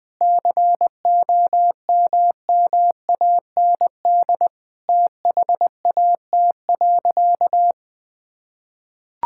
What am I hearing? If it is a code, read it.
Morse code, heard as a keyed tone, about 21 wpm, 700 Hz. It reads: COMMAND THAT.
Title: Sentences from Top 1600 Words 20wpm